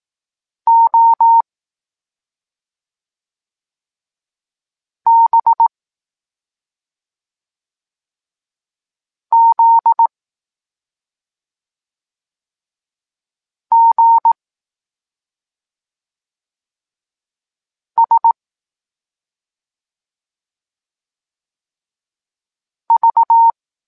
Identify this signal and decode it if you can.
Morse code, heard as a keyed tone, about 18 wpm, 920 Hz. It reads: O B Z G S V
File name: Tekst-23 4